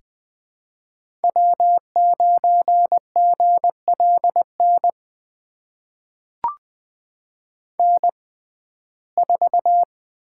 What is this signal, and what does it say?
Morse code, heard as a keyed tone, about 20 wpm, spaced as for 14 wpm, 700 Hz. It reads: W9GLN N4
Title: USA Call Signs - 5 Characters - 6x Character Spacing - 20wpm